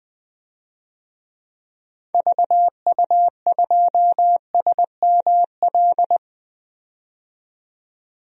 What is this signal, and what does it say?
Morse code, heard as a keyed tone, about 20 wpm, 700 Hz. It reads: VU2SML
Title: DXCC-100 Call Signs - 6 Characters - 20wpm